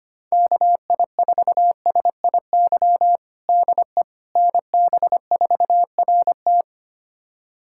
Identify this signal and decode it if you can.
Morse code, heard as a keyed tone, about 25 wpm, 700 Hz. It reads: KI4SIY DE NB4RT
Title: USA Callsign DE Callsign 25wpm